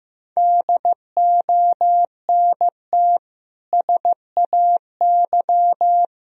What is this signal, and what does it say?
Morse code, heard as a keyed tone, about 15 wpm, 700 Hz. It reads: DONT SAY
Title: Sentences from Top 1000 Words - Review Part 1 - 15wpm